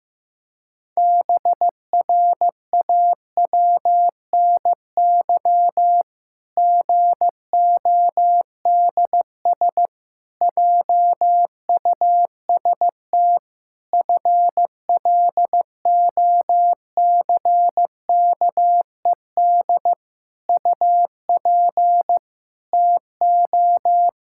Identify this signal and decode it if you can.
Morse code, heard as a keyed tone, about 15 wpm, 700 Hz. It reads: BRAWNY GODS JUST FLOCKED UP TO
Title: Pangram Challenge v2 - 15wpm